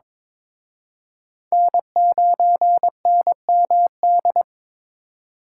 Morse code, heard as a keyed tone, about 26 wpm, 700 Hz. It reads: N9NMD
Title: Speed Racing - USA 1x3 Call Signs 22wpm